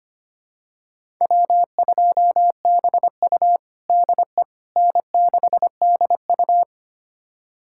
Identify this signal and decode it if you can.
Morse code, heard as a keyed tone, about 25 wpm, 700 Hz. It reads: W2BU DE N6DU
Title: USA Callsign DE Callsign - 4 Characters - 25wpm